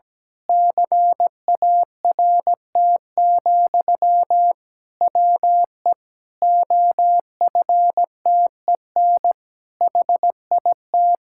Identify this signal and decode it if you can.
Morse code, heard as a keyed tone, about 17 wpm, 700 Hz. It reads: CART, WE OFTEN HIT
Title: Pangram Challenge v2 - 17wpm